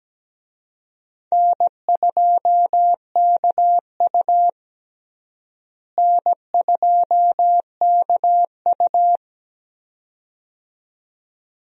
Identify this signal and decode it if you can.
Morse code, heard as a keyed tone, about 17 wpm, 700 Hz. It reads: N2KU N2KU
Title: USA Call Signs - 4 Characters - Repeated 2x - 17wpm